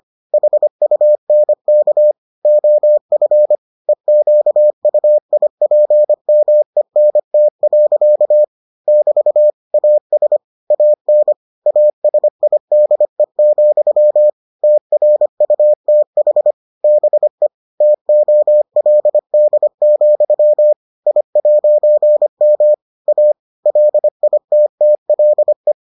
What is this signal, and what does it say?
Morse code, heard as a keyed tone, about 25 wpm, 600 Hz. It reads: HUNK OF EQUIPMENT. <BT> AS AN ASIDE, TRUTH BE TOLD, I'M A LITTLE